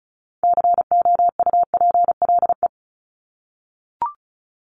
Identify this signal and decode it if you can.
Morse code, heard as a keyed tone, about 34 wpm, 700 Hz. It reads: COUPLE
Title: Top 1001-1250 Words 35wpm